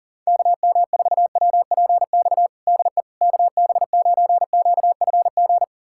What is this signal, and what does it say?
Morse code, heard as a keyed tone, about 40 wpm, 700 Hz. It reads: KM4WPX DE KB9QFG
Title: USA Callsign DE Callsign - 6 Characters - 40wpm